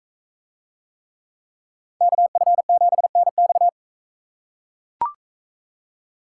Morse code, heard as a keyed tone, about 42 wpm, 700 Hz. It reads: KF7NX